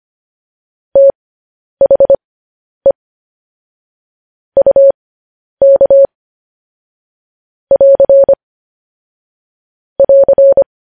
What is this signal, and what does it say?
Morse code, heard as a keyed tone, about 25 wpm, spaced as for 10 wpm, 550 Hz. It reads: THE UK <AR> <AR>
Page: News Headlines in Morse Code at 10 WPM